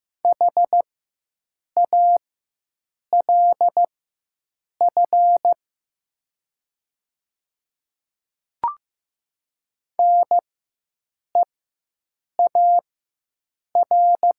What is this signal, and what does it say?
Morse code, heard as a keyed tone, about 15 wpm, spaced as for 7 wpm, 700 Hz. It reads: HALF NEAR
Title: 4x Character Spacing - Top 500 Words - 15wpm